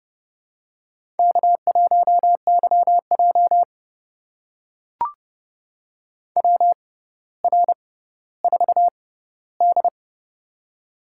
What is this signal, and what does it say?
Morse code, heard as a keyed tone, about 30 wpm, spaced as for 15 wpm, 700 Hz. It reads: K1YJ WR4D
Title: USA Call Signs - 4 Characters - 6x Character Spacing - 30wpm